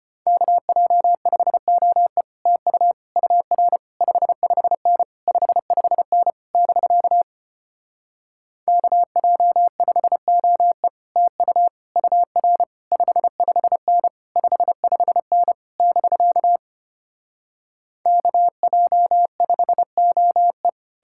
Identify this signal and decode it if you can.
Morse code, heard as a keyed tone, about 33 wpm, 700 Hz. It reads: KJ5OE TU UR 55N 55N <BK> KJ5OE TU UR 55N 55N <BK> KJ5OE